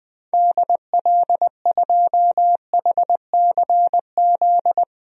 Morse code, heard as a keyed tone, about 20 wpm, 700 Hz. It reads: DL2HCZ
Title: DXCC-100 Call Signs - 6 Characters - 20wpm